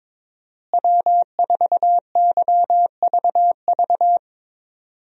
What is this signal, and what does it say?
Morse code, heard as a keyed tone, about 22 wpm, 700 Hz. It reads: W4YVV
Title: USA Call Signs - 5 Characters - Repeated 2x - 22wpm